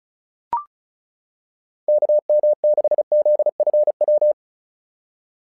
Morse code, heard as a keyed tone, about 36 wpm, 600 Hz. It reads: KM6ZFW
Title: SOTA QSO Practice 35wpm